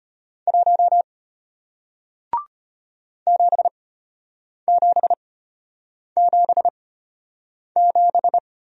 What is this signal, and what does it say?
Morse code, heard as a keyed tone, about 33 wpm, spaced as for 8 wpm, 700 Hz. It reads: 1 7777